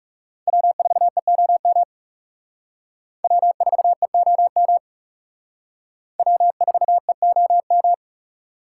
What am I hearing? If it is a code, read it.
Morse code, heard as a keyed tone, about 37 wpm, 700 Hz. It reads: W4EOM W4EOM W4EOM
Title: Speed Racing - USA 1x3 Call Signs 30wpm